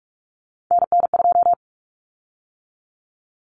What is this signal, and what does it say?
Morse code, heard as a keyed tone, about 44 wpm, 700 Hz. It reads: NN2
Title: Common Cut Numbers - 3-Digit 45wpm